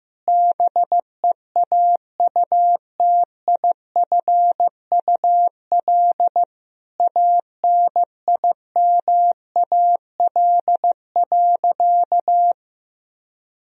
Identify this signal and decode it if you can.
Morse code, heard as a keyed tone, about 15 wpm, 700 Hz. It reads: BEAUTIFUL ANIMAL.
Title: Sentences from Top 1000 Words - Review Part 1 - 15wpm